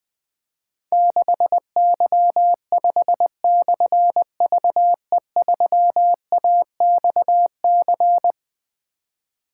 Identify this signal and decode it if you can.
Morse code, heard as a keyed tone, about 20 wpm, 700 Hz. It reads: 6Y5/VE3AXC
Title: Difficult Call Signs 20wpm